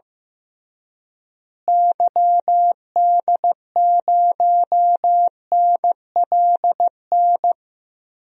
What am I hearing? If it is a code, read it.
Morse code, heard as a keyed tone, about 15 wpm, 700 Hz. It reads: YD0NLN
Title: DXCC-100 Call Signs - 6 Characters - Repeated 3x - 15wpm